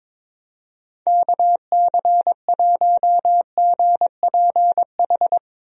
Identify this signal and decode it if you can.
Morse code, heard as a keyed tone, about 22 wpm, 700 Hz. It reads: KC1GPH
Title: USA Call Signs - 6 Characters - Repeated 3x - 22wpm